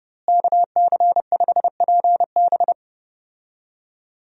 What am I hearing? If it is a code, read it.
Morse code, heard as a keyed tone, about 30 wpm, 700 Hz. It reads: KC5PB
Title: USA Call Signs - 5 Characters - Repeated 3x - 30wpm